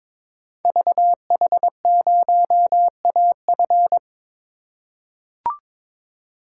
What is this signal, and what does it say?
Morse code, heard as a keyed tone, about 22 wpm, 700 Hz. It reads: VH0AF
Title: Code Groups - 5 Letter:Numbers - 22wpm